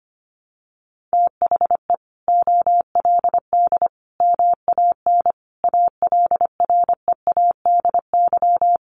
Morse code, heard as a keyed tone, about 25 wpm, 700 Hz. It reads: THE OLD MAN ALREADY
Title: Sentences from Top 700 Words 25wpm